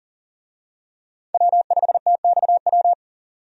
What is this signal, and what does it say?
Morse code, heard as a keyed tone, about 40 wpm, 700 Hz. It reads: W5TXW
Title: USA Call Signs - 5 Characters - 40wpm